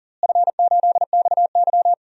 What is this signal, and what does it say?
Morse code, heard as a keyed tone, about 37 wpm, 700 Hz. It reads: F8XY
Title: DXCC-100 Call Signs - 4 Characters - Repeated 2x - 40wpm